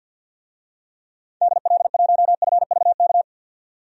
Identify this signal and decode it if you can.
Morse code, heard as a keyed tone, about 51 wpm, 700 Hz. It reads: DL1FVK
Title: DXCC-100 Call Signs - 6 Characters - 50wpm